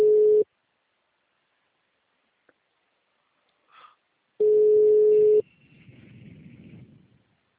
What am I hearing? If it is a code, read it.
Ringback tone (Europe, 425 Hz).
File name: out- -2016- - - .4